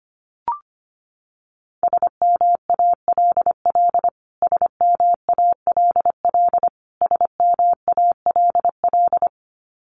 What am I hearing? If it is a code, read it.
Morse code, heard as a keyed tone, about 25 wpm, 700 Hz. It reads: SMALL SMALL SMALL
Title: Top 500 Words Repeated 25wpm